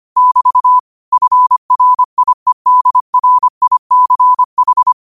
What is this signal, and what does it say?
Morse code, heard as a keyed tone, about 25 wpm, 1000 Hz. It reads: <BT> FRIEDRICH